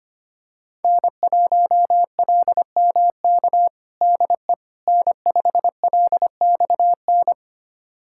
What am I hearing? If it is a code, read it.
Morse code, heard as a keyed tone, about 25 wpm, 700 Hz. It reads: N1LMK DE N5LXN
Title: USA Callsign DE Callsign - 5 Characters - 25wpm